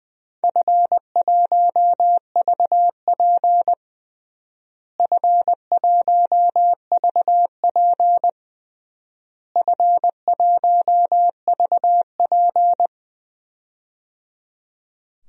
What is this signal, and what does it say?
Morse code, heard as a keyed tone, about 20 wpm, 700 Hz. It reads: F1VP F1VP F1VP
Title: DXCC-100 Call Signs - 4 Characters - Repeated 3x - 20wpm